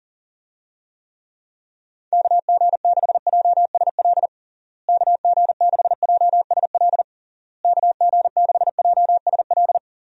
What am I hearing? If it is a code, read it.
Morse code, heard as a keyed tone, about 40 wpm, 700 Hz. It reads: KG6JSL KG6JSL KG6JSL
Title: USA Call Signs - 6 Characters - Repeated 3x - 40wpm